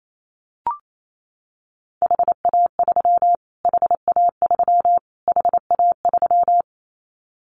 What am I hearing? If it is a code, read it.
Morse code, heard as a keyed tone, about 28 wpm, 700 Hz. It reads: HA3 HA3 HA3
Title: Callsign Prefixes for Top 25 DXCC - Repeated 3x - 28wpm